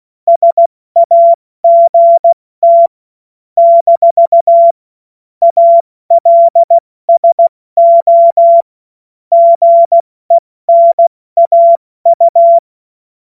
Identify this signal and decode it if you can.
Morse code, heard as a keyed tone, about 16 wpm, 680 Hz. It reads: SAGT - ALSO GENAU
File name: Morsekurs / Witze / Witz 21. Dezember 2021, 70 F80